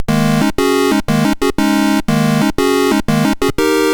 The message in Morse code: none